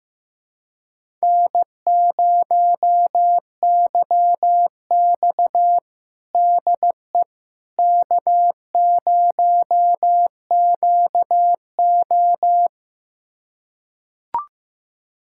Morse code, heard as a keyed tone, about 15 wpm, 700 Hz. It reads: N0YX DE K0QO
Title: USA Callsign DE Callsign - 4 Characters - 15wpm